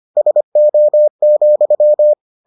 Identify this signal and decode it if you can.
Morse code, heard as a keyed tone, about 25 wpm, 600 Hz. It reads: SO,